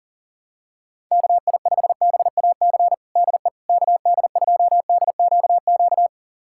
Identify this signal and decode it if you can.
Morse code, heard as a keyed tone, about 40 wpm, 700 Hz. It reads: KI5BAC DE KD2DQQ
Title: USA Callsign DE Callsign - 6 Characters - 40wpm